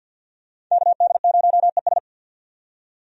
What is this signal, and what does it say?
Morse code, heard as a keyed tone, about 51 wpm, 700 Hz. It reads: KD0ES
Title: USA Call Signs - 5 Characters - 50wpm